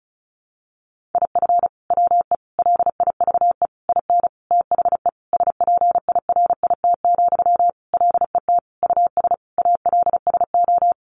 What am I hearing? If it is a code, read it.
Morse code, heard as a keyed tone, about 34 wpm, 700 Hz. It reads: IF WE LIVE IN THE SPIRIT, LET US ALSO
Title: Sentences from Top 1400 Words 35wpm